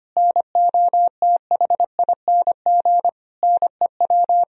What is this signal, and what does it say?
Morse code, heard as a keyed tone, about 25 wpm, 700 Hz. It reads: NOTHING NEW
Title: Sentences from Top 1000 Words - Review Part 1 - 25wpm